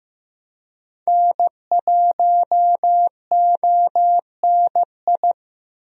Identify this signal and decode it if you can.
Morse code, heard as a keyed tone, about 15 wpm, 700 Hz. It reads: N1ONI
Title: USA Call Signs - 5 Characters - Repeated 3x - 15wpm